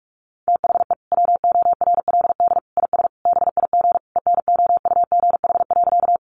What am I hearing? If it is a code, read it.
Morse code, heard as a keyed tone, about 46 wpm, 700 Hz. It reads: THE WORLD IS BIG ENOUGH.
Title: Sentences from Top 400 Words 45wpm